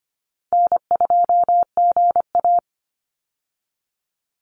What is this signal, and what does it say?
Morse code, heard as a keyed tone, about 25 wpm, 700 Hz. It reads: N2GA
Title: Call Signs - 25wpm